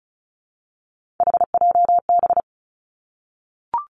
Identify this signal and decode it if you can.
Morse code, heard as a keyed tone, about 34 wpm, 700 Hz. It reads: HJB
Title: Three Challenging Letter-Number Combinations 35wpm